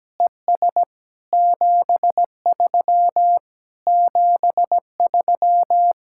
Morse code, heard as a keyed tone, about 17 wpm, 700 Hz. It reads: ES 73 73
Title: Speed-Racing Common QSO Phrases - 17wpm